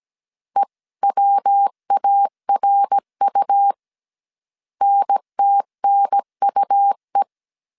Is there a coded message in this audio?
Yes: Morse code from 0.6 to 7.2 s, heard as a keyed tone, about 17 wpm, 800 Hz. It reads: EWARU NTNUE